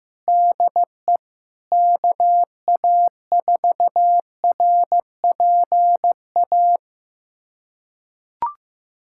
Morse code, heard as a keyed tone, about 16 wpm, 700 Hz. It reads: DE KA4RPA
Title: USA Callsign DE Callsign 15wpm